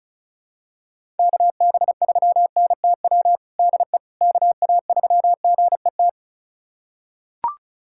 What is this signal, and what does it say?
Morse code, heard as a keyed tone, about 36 wpm, 700 Hz. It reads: KB3NTW DE KA3GET
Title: USA Callsign DE Callsign - 6 Characters - 35wpm